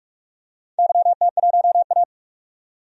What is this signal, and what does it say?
Morse code, heard as a keyed tone, about 42 wpm, 700 Hz. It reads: YT1A
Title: DXCC-100 Call Signs - 4 Characters - 45wpm